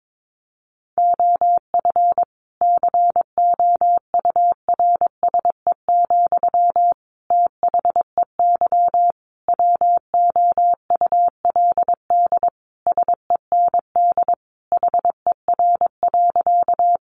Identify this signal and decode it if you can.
Morse code, heard as a keyed tone, about 22 wpm, 700 Hz. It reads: OF COURSE, THEY WOULD SEND HER.
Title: Sentences from Top 1500 Words 22wpm